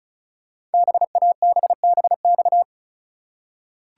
Morse code, heard as a keyed tone, about 36 wpm, 690 Hz. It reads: BABBX
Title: Code Groups - 5 Letters - 35wpm